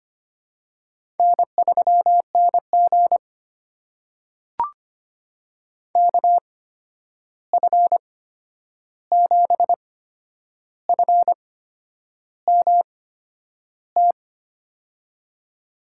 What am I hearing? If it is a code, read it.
Morse code, heard as a keyed tone, about 25 wpm, spaced as for 9 wpm, 700 Hz. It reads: N3NG KF7FMT
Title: USA Call Signs - 8x Character Spacing - 25wpm